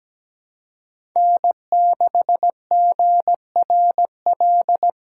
Morse code, heard as a keyed tone, about 17 wpm, 700 Hz. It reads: N6GRL
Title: USA Call Signs - 5 Characters - 3x Character Spacing - 17wpm